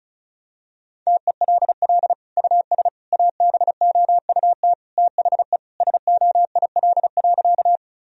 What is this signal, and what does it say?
Morse code, heard as a keyed tone, about 34 wpm, 700 Hz. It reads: TELL US ABOUT THE SOIL.
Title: Sentences from Top 1000 Words - Review Part 4 - 35wpm